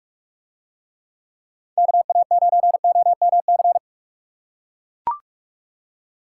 Morse code, heard as a keyed tone, about 44 wpm, 700 Hz. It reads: KA9OMC